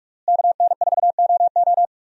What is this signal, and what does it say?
Morse code, heard as a keyed tone, about 46 wpm, 700 Hz. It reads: KN4OO
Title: USA Call Signs - 5 Characters - Repeated 3x - 45wpm